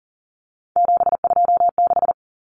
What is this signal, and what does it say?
Morse code, heard as a keyed tone, about 40 wpm, 700 Hz. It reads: 726
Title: Common Cut Numbers - 3-Digit 40wpm